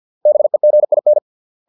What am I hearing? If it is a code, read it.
Morse code, heard as a keyed tone, about 50 wpm, 600 Hz. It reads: BEGIN